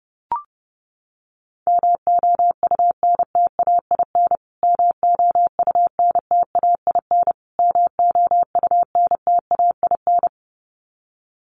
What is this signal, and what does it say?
Morse code, heard as a keyed tone, about 30 wpm, 700 Hz. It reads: MOUNTAIN MOUNTAIN MOUNTAIN
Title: Top 500 Words Repeated 30wpm